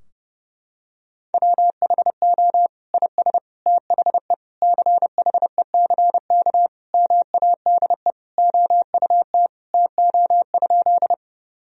Morse code, heard as a keyed tone, about 30 wpm, 700 Hz. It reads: WHO IS THE CHECK MADE OUT TO?
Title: Sentences from Top 1000 Words - Review Part 3 - 30wpm